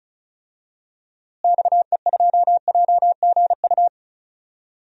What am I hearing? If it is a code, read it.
Morse code, heard as a keyed tone, about 34 wpm, 700 Hz. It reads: XE2JGU
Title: DXCC-100 Call Signs - 6 Characters - Repeated 2x - 35wpm